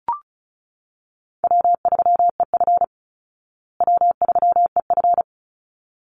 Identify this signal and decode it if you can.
Morse code, heard as a keyed tone, about 34 wpm, 700 Hz. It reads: W3EF W3EF
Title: Call Signs Repeated 3x - 35wpm